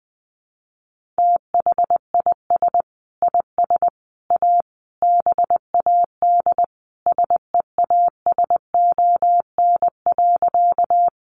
Morse code, heard as a keyed tone, about 20 wpm, 700 Hz. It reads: THIS IS A BAD SEASON.
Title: Sentences from Top 1400 Words 20wpm